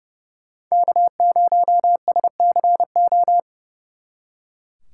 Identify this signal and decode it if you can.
Morse code, heard as a keyed tone, about 35 wpm, 700 Hz. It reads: K0SCO
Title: Speed Racing - USA 1x3 Call Signs 30wpm